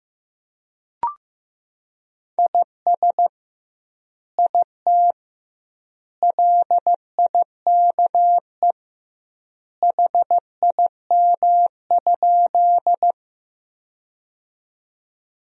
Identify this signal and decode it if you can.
Morse code, heard as a keyed tone, about 16 wpm, 700 Hz. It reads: IS IT LIKE HIM?
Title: 2x Word-Spacing - Sentences from Top 100 Words - 15wpm